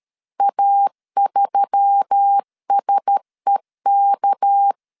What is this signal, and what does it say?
Morse code, heard as a keyed tone, about 13 wpm, 800 Hz. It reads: A3SEK
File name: LEZIONI / 6 PKX0 / 6 PK 12wpm 2006